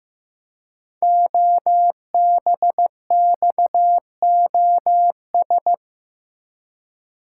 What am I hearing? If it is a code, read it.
Morse code, heard as a keyed tone, about 15 wpm, 700 Hz. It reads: OBXOS
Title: Code Groups - 5 Letters - 15wpm